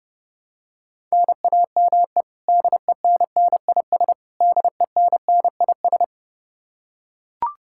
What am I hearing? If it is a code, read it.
Morse code, heard as a keyed tone, about 30 wpm, 700 Hz. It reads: NAME DENNIS DENNIS
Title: Practice QSO - Exchange 1 - Beginning - 30wpm